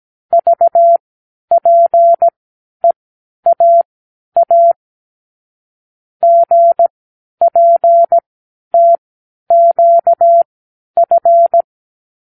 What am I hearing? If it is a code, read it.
Morse code, heard as a keyed tone, about 17 wpm, spaced as for 10 wpm, 680 Hz. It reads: VPEAA GPTQF
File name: k7qo / Lesson 36 (A-X)